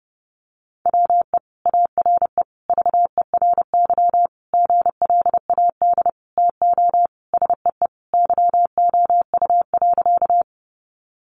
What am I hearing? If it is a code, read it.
Morse code, heard as a keyed tone, about 30 wpm, 700 Hz. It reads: WE ARE VERY GLAD TO SEE YOU.
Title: Sentences from Top 800 Words 30wpm